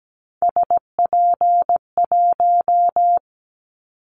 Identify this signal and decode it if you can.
Morse code, heard as a keyed tone, about 17 wpm, 700 Hz. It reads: SP1
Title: Callsign Prefixes for Top 25 DXCC - 17wpm